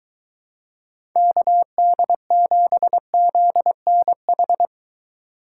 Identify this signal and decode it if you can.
Morse code, heard as a keyed tone, about 23 wpm, 700 Hz. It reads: KD7ZNH